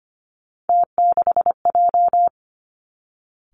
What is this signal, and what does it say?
Morse code, heard as a keyed tone, about 25 wpm, 700 Hz. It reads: T6J
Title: Three Challenging Letter-Number Combinations 25wpm